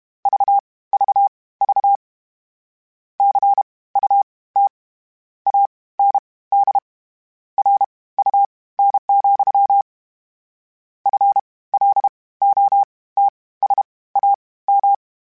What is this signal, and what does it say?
Morse code, heard as a keyed tone, about 32 wpm, spaced as for 19 wpm, 800 Hz. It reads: VVV CUT AND RUN, FLOTSAM
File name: Misc / Misc-mp3s / WordPairs sp @30